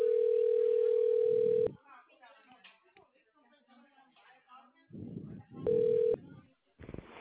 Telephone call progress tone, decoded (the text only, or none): ringback tone